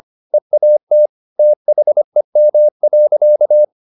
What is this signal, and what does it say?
Morse code, heard as a keyed tone, about 25 wpm, 600 Hz. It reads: EAT THEM.